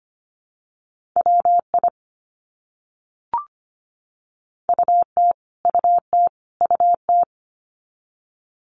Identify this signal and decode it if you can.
Morse code, heard as a keyed tone, about 25 wpm, 700 Hz. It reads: WI UT UT UT
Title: US States Repeated 3x 25wpm